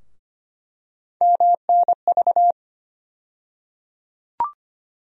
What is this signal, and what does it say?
Morse code, heard as a keyed tone, about 25 wpm, 700 Hz. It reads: MNV
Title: USA Call Sign Postfixes - 3 Letters 25wpm